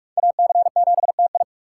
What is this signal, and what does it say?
Morse code, heard as a keyed tone, about 44 wpm, 700 Hz. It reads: AC7TI